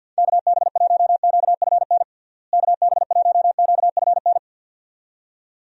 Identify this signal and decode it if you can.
Morse code, heard as a keyed tone, about 49 wpm, 700 Hz. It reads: KB1QFN KB1QFN
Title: USA Call Signs - 6 Characters - Repeated 2x - 50wpm